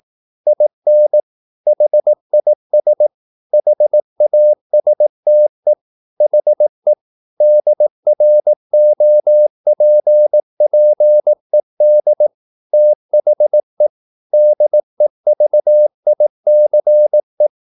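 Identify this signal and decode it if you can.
Morse code, heard as a keyed tone, about 18 wpm, 600 Hz. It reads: IN HIS HASTE HE DROPPED THE DEVICE
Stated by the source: Morse practice at 18 wpm, English text, from CW med Gnister.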